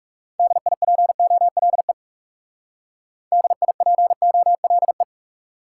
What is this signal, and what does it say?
Morse code, heard as a keyed tone, about 38 wpm, 700 Hz. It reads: DIPOLE DIPOLE